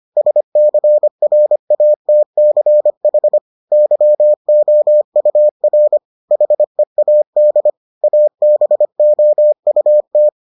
Morse code, heard as a keyed tone, about 25 wpm, 600 Hz. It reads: SCRATCH YOUR HEAD ABOUT